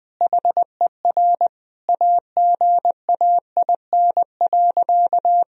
Morse code, heard as a keyed tone, about 20 wpm, 700 Hz. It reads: HER AGAIN.
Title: Sentences from Top 1000 Words - Review Part 4 - 20wpm